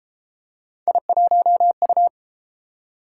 Morse code, heard as a keyed tone, about 33 wpm, 700 Hz. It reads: I1U